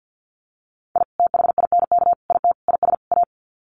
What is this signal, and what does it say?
Morse code, heard as a keyed tone, about 48 wpm, 700 Hz. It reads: I THINK IT IS A
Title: Sets of 5 Words Encore 50wpm